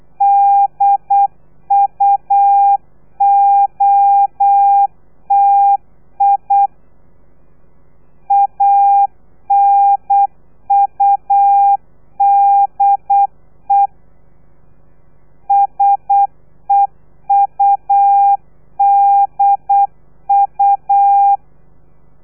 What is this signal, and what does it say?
Morse code, heard as a keyed tone, about 8 wpm, 800 Hz. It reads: DUOTI ANUDE SEUDU
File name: telegrafia / i2tfj / Lez3 35